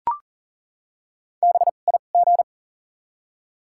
Morse code, heard as a keyed tone, about 40 wpm, 700 Hz. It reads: B I G
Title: 2x Character Spacing - Top 200 Words - 40wpm